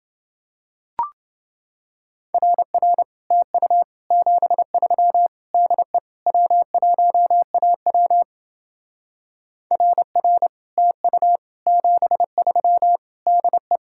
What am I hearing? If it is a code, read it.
Morse code, heard as a keyed tone, about 29 wpm, 700 Hz. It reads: RR TU 73 DE W1AW RR TU 73 DE